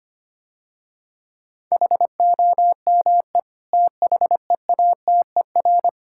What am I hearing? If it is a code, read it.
Morse code, heard as a keyed tone, about 25 wpm, 690 Hz. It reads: HOME THEATER
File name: practice / 25wpm / Difficult Words - Speed-Racing with Hints - 25wpm